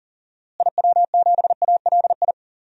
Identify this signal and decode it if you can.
Morse code, heard as a keyed tone, about 40 wpm, 700 Hz. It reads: IW7ALI
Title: DXCC-100 Call Signs - 6 Characters - Repeated 3x - 40wpm